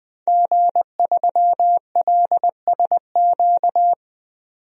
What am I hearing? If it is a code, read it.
Morse code, heard as a keyed tone, about 20 wpm, 700 Hz. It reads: G3LSQ
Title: DXCC-100 Call Signs - 5 Characters - 20wpm